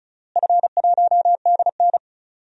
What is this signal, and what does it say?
Morse code, heard as a keyed tone, about 36 wpm, 700 Hz. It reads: F1DN
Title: DXCC-100 Call Signs - 4 Characters - Repeated 3x - 35wpm